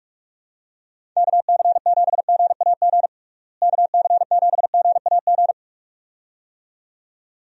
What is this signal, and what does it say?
Morse code, heard as a keyed tone, about 46 wpm, 700 Hz. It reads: KC7GAG KC7GAG
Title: USA Call Signs - 6 Characters - Repeated 2x - 45wpm